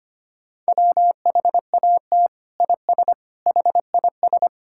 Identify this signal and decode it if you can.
Morse code, heard as a keyed tone, about 25 wpm, 700 Hz. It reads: WHAT IS HIS
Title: Sentences from Top 1000 Words - Review Part 5 - 25wpm